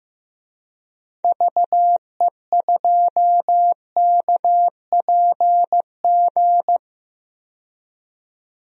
Morse code, heard as a keyed tone, about 15 wpm, 700 Hz. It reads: VE2KPG
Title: DXCC-100 Call Signs - 6 Characters - 15wpm